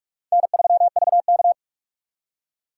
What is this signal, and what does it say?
Morse code, heard as a keyed tone, about 44 wpm, 700 Hz. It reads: N3VK